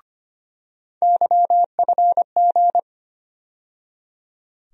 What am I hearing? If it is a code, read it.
Morse code, heard as a keyed tone, about 25 wpm, 700 Hz. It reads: YFG
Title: USA Call Sign Postfixes - 3 Letters 25wpm